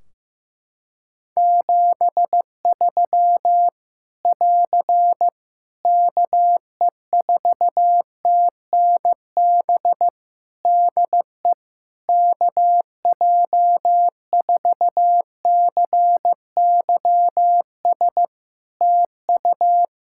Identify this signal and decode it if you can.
Morse code, heard as a keyed tone, about 15 wpm, 700 Hz. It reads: 73 <AR> KE4TNB DE KJ4CYS TU
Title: Practice QSO - Exchange 4 - Ending - 15wpm